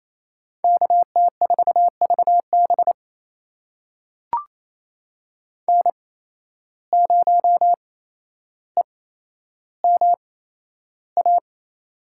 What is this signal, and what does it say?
Morse code, heard as a keyed tone, about 28 wpm, spaced as for 12 wpm, 700 Hz. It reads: KT4VB N0EMA
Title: USA Call Signs - 5 Characters - 8x Character Spacing - 28wpm